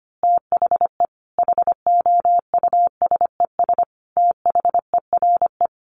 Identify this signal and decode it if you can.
Morse code, heard as a keyed tone, about 25 wpm, 700 Hz. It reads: THE HOUSES THERE
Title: Easy Phrases 25wpm